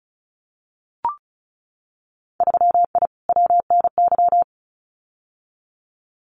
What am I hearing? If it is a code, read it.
Morse code, heard as a keyed tone, about 36 wpm, 700 Hz. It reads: 3I WNY
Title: Winter Field Day Exchanges Encore 35wpm